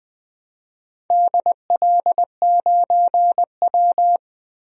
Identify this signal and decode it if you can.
Morse code, heard as a keyed tone, about 20 wpm, 700 Hz. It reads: DL9W
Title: DXCC-100 Call Signs - 4 Characters - 20wpm